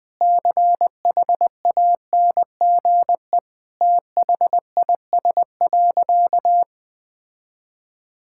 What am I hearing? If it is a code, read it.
Morse code, heard as a keyed tone, about 21 wpm, 700 Hz. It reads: CHANGE THIS.
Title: Sentences from Top 1000 Words - Review Part 6 - 20wpm